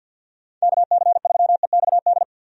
Morse code, heard as a keyed tone, about 50 wpm, 700 Hz. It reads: KC3EXD
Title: USA Call Signs - 6 Characters - 50wpm